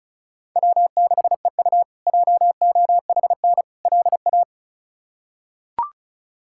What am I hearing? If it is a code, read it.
Morse code, heard as a keyed tone, about 34 wpm, 700 Hz. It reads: W6EU JOHN LA
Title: K1USN SST QSO Practice - 35wpm